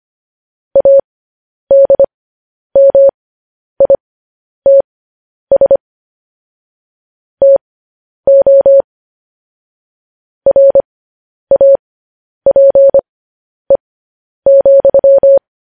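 Morse code, heard as a keyed tone, about 25 wpm, spaced as for 10 wpm, 550 Hz. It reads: ADMITS TO RAPE,
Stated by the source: Morse news headlines sent at 10 words per minute.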